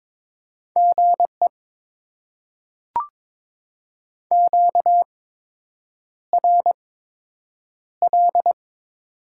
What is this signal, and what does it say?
Morse code, heard as a keyed tone, about 22 wpm, spaced as for 8 wpm, 700 Hz. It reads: GE QRL
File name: Top 20 QSO Elements - 8x Character Spacing - 22wpm